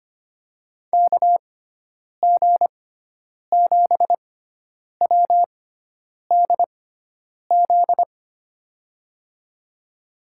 Morse code, heard as a keyed tone, about 25 wpm, spaced as for 9 wpm, 700 Hz. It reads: KG7WDZ
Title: USA Call Signs - 6 Characters - 6x Character Spacing - 25wpm